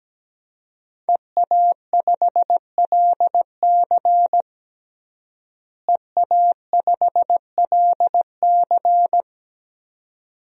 Morse code, heard as a keyed tone, about 17 wpm, 700 Hz. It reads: EA5LC EA5LC
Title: DXCC-100 Call Signs - 5 Characters - Repeated 2x - 17wpm